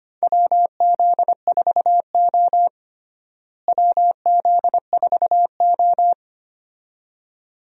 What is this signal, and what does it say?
Morse code, heard as a keyed tone, about 25 wpm, 700 Hz. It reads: WZ4O WZ4O
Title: USA Call Signs - 4 Characters - Repeated 2x - 25wpm